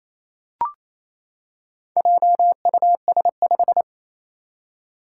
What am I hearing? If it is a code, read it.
Morse code, heard as a keyed tone, about 27 wpm, 700 Hz. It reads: JUS5
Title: Four Challenging Letter-Number Combinations 28wpm